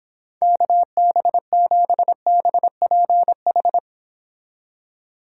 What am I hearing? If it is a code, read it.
Morse code, heard as a keyed tone, about 26 wpm, 700 Hz. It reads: KB7BPH